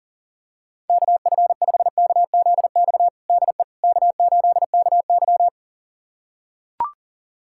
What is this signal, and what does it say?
Morse code, heard as a keyed tone, about 37 wpm, 700 Hz. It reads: KF5KZX DE K8KY
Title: USA Callsign DE Callsign 40wpm